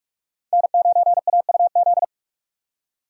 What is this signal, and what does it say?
Morse code, heard as a keyed tone, about 42 wpm, 700 Hz. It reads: N9AUZ
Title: USA Call Signs - 5 Characters - 2x Character Spacing - 45wpm